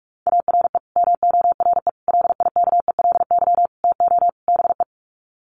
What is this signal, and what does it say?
Morse code, heard as a keyed tone, about 44 wpm, 700 Hz. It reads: ARE MORE LIKELY TO BE
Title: Sets of 5 Words 45wpm